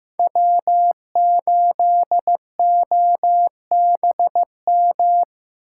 Morse code, heard as a keyed tone, about 18 wpm, 700 Hz. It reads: W8OBM
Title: Speed Racing - USA 1x3 Call Signs 15wpm